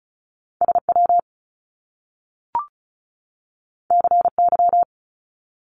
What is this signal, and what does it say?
Morse code, heard as a keyed tone, about 36 wpm, 700 Hz. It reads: S W C Y
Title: Top Two Letter Combinations in Words 35wpm